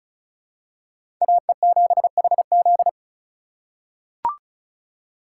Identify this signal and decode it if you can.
Morse code, heard as a keyed tone, about 34 wpm, 700 Hz. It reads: AE7HZ
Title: USA Call Signs - 5 Characters - 4x Character Spacing - 35wpm